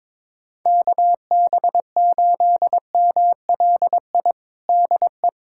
Morse code, heard as a keyed tone, about 22 wpm, 700 Hz. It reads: KB8MLI DE
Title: USA Callsign DE Callsign 22wpm